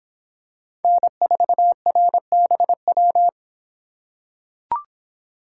Morse code, heard as a keyed tone, about 26 wpm, 700 Hz. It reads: N4RBW